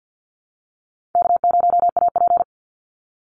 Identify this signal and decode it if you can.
Morse code, heard as a keyed tone, about 50 wpm, 700 Hz. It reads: K0AP
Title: Call Signs - 50wpm